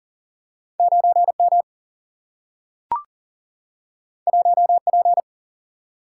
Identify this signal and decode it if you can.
Morse code, heard as a keyed tone, about 40 wpm, 700 Hz. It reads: 9 M 1 P
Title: Single-Letter Single-Number 40wpm